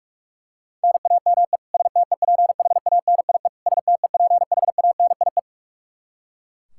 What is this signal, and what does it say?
Morse code, heard as a keyed tone, about 41 wpm, 700 Hz. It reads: NAME STEPHANIE STEPHANIE
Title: Practice QSO - Exchange 1 - Beginning - 45wpm